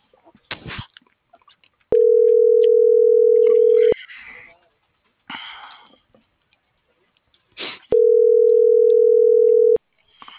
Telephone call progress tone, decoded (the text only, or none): ringback tone